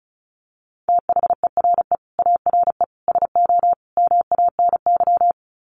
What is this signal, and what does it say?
Morse code, heard as a keyed tone, about 34 wpm, 700 Hz. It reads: THERE ARE SO MANY
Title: Sets of 4 Words 35wpm